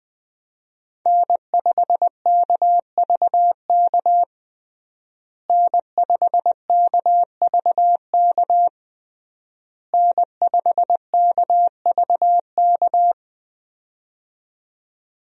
Morse code, heard as a keyed tone, about 20 wpm, 700 Hz. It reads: N5KVK N5KVK N5KVK
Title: USA Call Signs - 5 Characters - Repeated 3x - 20wpm